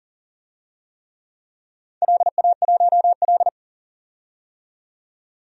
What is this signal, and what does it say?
Morse code, heard as a keyed tone, about 40 wpm, 700 Hz. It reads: LA1L
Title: DXCC-100 Call Signs - 4 Characters - 40wpm